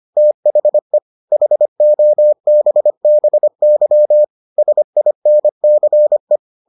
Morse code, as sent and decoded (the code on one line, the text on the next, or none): - .... / .... --- -... -... -.-- / ... .. -. -.-. .
THE HOBBY SINCE